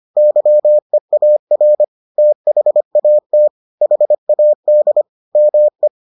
Morse code, heard as a keyed tone, about 25 wpm, 600 Hz. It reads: YEAR THAT HAD ME